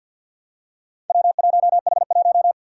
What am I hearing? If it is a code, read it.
Morse code, heard as a keyed tone, about 45 wpm, 700 Hz. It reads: W1H1
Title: Four Challenging Letter-Number Combinations 50wpm